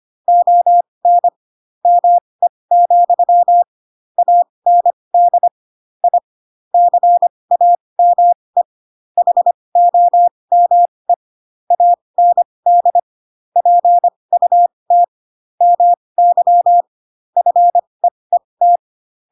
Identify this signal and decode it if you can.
Morse code, heard as a keyed tone, about 25 wpm, spaced as for 20 wpm, 700 Hz. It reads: ON ME, AND I CAME HOME AND PUT MY FEET
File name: Pinocchio / Pinocchio ch7 15 25 ik1hge